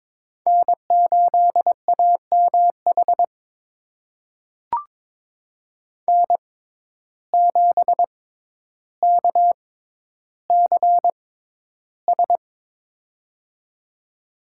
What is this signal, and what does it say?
Morse code, heard as a keyed tone, about 22 wpm, spaced as for 11 wpm, 700 Hz. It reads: N8AMH N7KCS
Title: USA Call Signs - 5 Characters - 6x Character Spacing - 22wpm